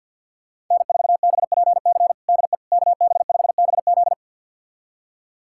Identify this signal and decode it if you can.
Morse code, heard as a keyed tone, about 46 wpm, 700 Hz. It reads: N4BPC DE KB5BZ
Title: USA Callsign DE Callsign - 5 Characters - 50wpm